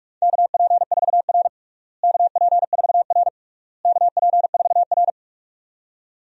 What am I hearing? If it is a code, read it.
Morse code, heard as a keyed tone, about 46 wpm, 690 Hz. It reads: KP4R KP4R KP4R
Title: DXCC-100 Call Signs - 4 Characters - Repeated 3x - 45wpm